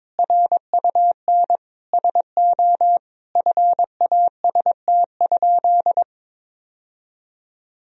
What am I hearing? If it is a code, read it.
Morse code, heard as a keyed tone, about 22 wpm, 700 Hz. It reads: RUN SO FAST?
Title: Sentences from Top 1000 Words - Review Part 2 - 22wpm